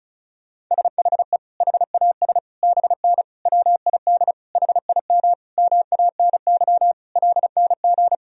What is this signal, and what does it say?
Morse code, heard as a keyed tone, about 34 wpm, 700 Hz. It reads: SHE HAS BN WID HIM MANY LNG
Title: Sentences from Top 200 Words and Abbreviations 35wpm